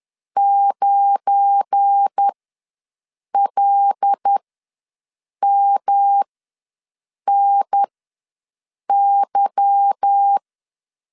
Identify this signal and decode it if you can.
Morse code, heard as a keyed tone, about 10 wpm, spaced as for 6 wpm, 800 Hz. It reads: 9LMNY
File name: LEZIONI / 9 2 ws 10wpm 4130